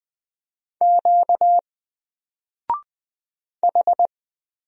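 Morse code, heard as a keyed tone, about 20 wpm, 700 Hz. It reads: Q H
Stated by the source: Single letter and number Morse drill at 20 wpm, from N4LCD.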